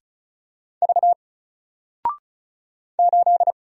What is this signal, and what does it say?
Morse code, heard as a keyed tone, about 35 wpm, 700 Hz. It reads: V 8
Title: Single Character (Letter:Number:Punctuation:Prosign) 35wpm